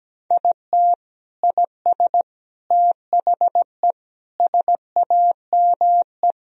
Morse code, heard as a keyed tone, about 17 wpm, 700 Hz. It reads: IT IS THE SAME
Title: Sentences from Top 1000 Words - Review Part 3 - 17wpm